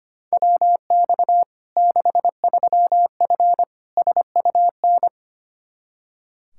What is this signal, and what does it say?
Morse code, heard as a keyed tone, about 25 wpm, 700 Hz. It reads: WX 63F SUN
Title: Practice QSO - Exchange 2 - Info - 25wpm